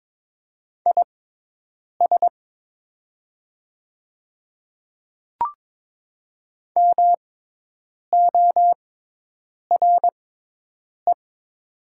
Morse code, heard as a keyed tone, about 22 wpm, spaced as for 8 wpm, 700 Hz. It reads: IS MORE
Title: 6x Character Spacing - Top 100 Words - 22wpm